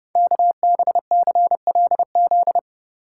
Morse code, heard as a keyed tone, about 30 wpm, 700 Hz. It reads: KBCLZ
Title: Code Groups - 5 Letters - 30wpm